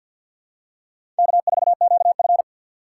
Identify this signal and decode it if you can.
Morse code, heard as a keyed tone, about 51 wpm, 700 Hz. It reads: K4QF
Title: USA Call Signs - 6x Character Spacing - 50wpm